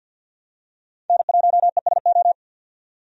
Morse code, heard as a keyed tone, about 51 wpm, 700 Hz. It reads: N1ESO
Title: USA Call Signs - 5 Characters - 50wpm